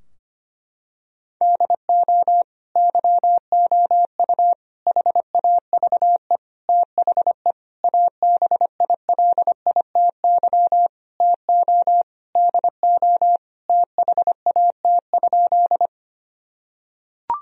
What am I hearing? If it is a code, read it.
Morse code, heard as a keyed tone, about 25 wpm, 700 Hz. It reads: DO YOU HAVE THE ABILITY TO DO THAT?
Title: Sentences from Top 1000 Words - Review Part 3 - 25wpm